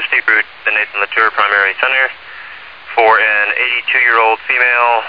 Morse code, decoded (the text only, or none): none